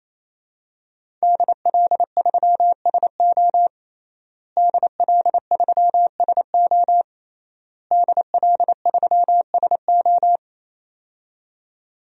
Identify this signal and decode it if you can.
Morse code, heard as a keyed tone, about 28 wpm, 700 Hz. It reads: DL3SO DL3SO DL3SO
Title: DXCC-100 Call Signs - 5 Characters - Repeated 3x - 28wpm